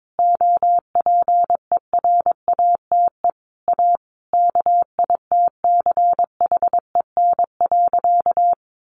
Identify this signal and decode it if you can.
Morse code, heard as a keyed tone, about 22 wpm, 700 Hz. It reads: OPERATE A KITCHEN.
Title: Sentences from Top 1500 Words 22wpm